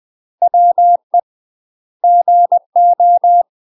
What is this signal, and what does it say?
Morse code, heard as a keyed tone, about 20 wpm, 700 Hz. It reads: WE GO
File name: UK MASTER QSOs TRAINING / 20WPM / UK MASTER QSOs 0034